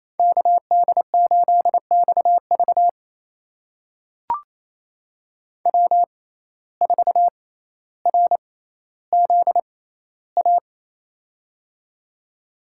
Morse code, heard as a keyed tone, about 27 wpm, spaced as for 14 wpm, 700 Hz. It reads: KD8XV W4RZA
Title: USA Call Signs - 5 Characters - 6x Character Spacing - 28wpm